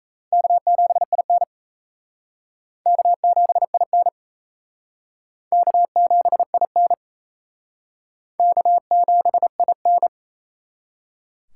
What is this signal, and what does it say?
Morse code, heard as a keyed tone, about 34 wpm, 700 Hz. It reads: K7IN K7IN K7IN K7IN
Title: Speed Racing - USA 1x2 Call Signs 28wpm